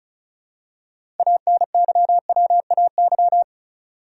Morse code, heard as a keyed tone, about 36 wpm, 700 Hz. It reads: ANYWAY
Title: Six Letter Words - 35wpm